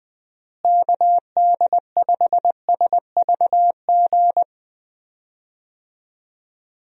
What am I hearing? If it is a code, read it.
Morse code, heard as a keyed tone, about 20 wpm, 700 Hz. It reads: KD5SVG
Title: USA Call Signs - 6 Characters - Repeated 2x - 20wpm